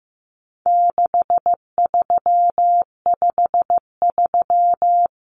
Morse code, heard as a keyed tone, about 15 wpm, 700 Hz. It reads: 6353
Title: Numbers - 4-Digit 15wpm